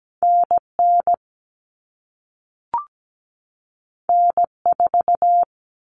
Morse code, heard as a keyed tone, about 17 wpm, 700 Hz. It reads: NN N4
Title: All Cut Numbers - 2-Digit 17wpm